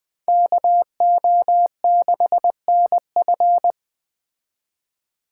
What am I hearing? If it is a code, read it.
Morse code, heard as a keyed tone, about 20 wpm, 700 Hz. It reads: KO6NF
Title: Speed Racing - USA 2x2 Call Signs 17wpm